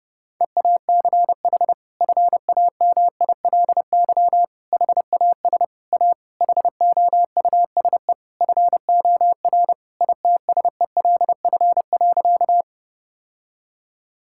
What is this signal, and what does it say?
Morse code, heard as a keyed tone, about 32 wpm, 700 Hz. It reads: EACH FAMILY HAS A HOUSE FOR ITSELF.
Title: Sentences from Top 1000 Words - Review Part 5 - 30wpm